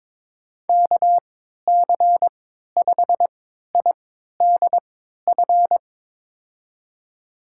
Morse code, heard as a keyed tone, about 21 wpm, spaced as for 12 wpm, 700 Hz. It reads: KC5IDF
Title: USA Call Signs - 3x Character Spacing - 22wpm